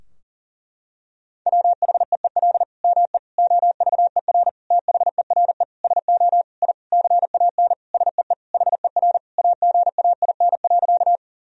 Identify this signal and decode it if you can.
Morse code, heard as a keyed tone, about 40 wpm, 700 Hz. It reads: WHEEL ME OVER THERE SO I CAN SEE HER AGAIN.
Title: Sentences from Top 1000 Words - Review Part 3 - 40wpm